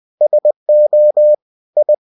Morse code, heard as a keyed tone, about 20 wpm, 600 Hz. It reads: SO I
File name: cw-books / winnie-the-pooh / winnie-20wpm-page-0024